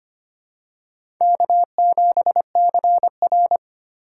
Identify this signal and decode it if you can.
Morse code, heard as a keyed tone, about 25 wpm, 700 Hz. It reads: K7CR
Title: USA Call Signs - 4 Characters - 4x Character Spacing - 25wpm